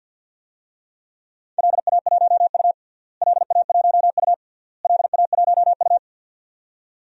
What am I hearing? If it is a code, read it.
Morse code, heard as a keyed tone, about 50 wpm, 700 Hz. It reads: LA1U LA1U LA1U
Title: DXCC-100 Call Signs - 4 Characters - Repeated 3x - 50wpm